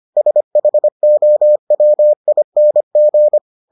Morse code, heard as a keyed tone, about 25 wpm, 600 Hz. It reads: SHOWING